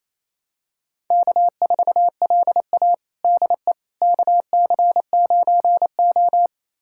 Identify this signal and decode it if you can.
Morse code, heard as a keyed tone, about 28 wpm, 700 Hz. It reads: K4LA DE KC9O
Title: USA Callsign DE Callsign - 4 Characters - 28wpm